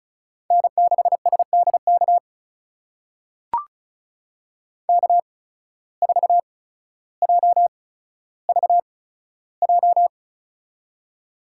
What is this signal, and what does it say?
Morse code, heard as a keyed tone, about 35 wpm, spaced as for 15 wpm, 690 Hz. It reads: N6SDK K4JVJ
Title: USA Call Signs - 5 Characters - 8x Character Spacing - 35wpm